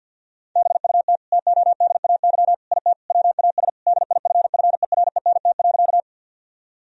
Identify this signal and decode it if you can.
Morse code, heard as a keyed tone, about 50 wpm, 700 Hz. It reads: BUT TODAY IT WAS DIFFERENT.
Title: Sentences from Top 1000 Words - Review Part 4 - 50wpm